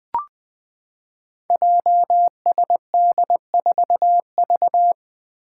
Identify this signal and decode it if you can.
Morse code, heard as a keyed tone, about 20 wpm, 700 Hz. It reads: JSD4V
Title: Five Challenging Letter-Number Combinations 20wpm